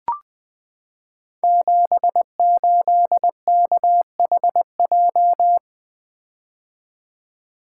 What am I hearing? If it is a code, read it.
Morse code, heard as a keyed tone, about 20 wpm, 700 Hz. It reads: 78KHJ
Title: Code Groups - 5 Letter:Numbers - 20wpm